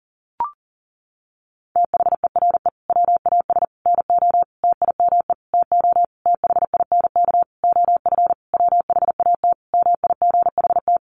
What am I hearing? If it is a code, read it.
Morse code, heard as a keyed tone, about 40 wpm, 700 Hz. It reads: THERE WAS NO TIME TO THINK OF WHAT MIGHT
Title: Sentences from Top 400 Words 40wpm